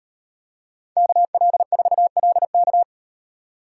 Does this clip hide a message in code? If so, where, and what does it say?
Yes: Morse code from 1.0 to 2.8 s, heard as a keyed tone, about 38 wpm, 700 Hz. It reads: KL4LK